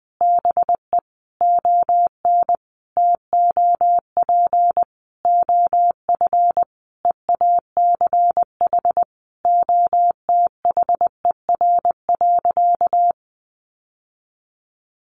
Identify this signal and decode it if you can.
Morse code, heard as a keyed tone, about 20 wpm, 700 Hz. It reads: BE ON TOP OF EACH OTHER.
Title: Sentences from Top 700 Words 20wpm